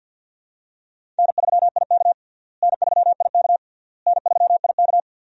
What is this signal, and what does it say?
Morse code, heard as a keyed tone, about 49 wpm, 700 Hz. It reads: N3IK N3IK N3IK
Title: USA Call Signs - 4 Characters - Repeated 3x - 50wpm